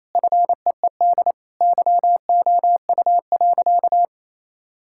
Morse code, heard as a keyed tone, about 28 wpm, 700 Hz. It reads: FEED YOU.
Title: Sentences from Top 1600 Words 28wpm